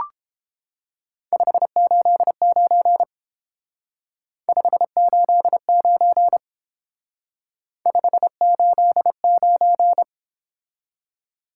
Morse code, heard as a keyed tone, about 31 wpm, 700 Hz. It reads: 589 589 589